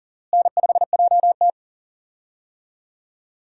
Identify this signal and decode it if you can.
Morse code, heard as a keyed tone, about 40 wpm, 700 Hz. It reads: N5JT
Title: USA Call Signs - 4 Characters - Repeated 3x - 40wpm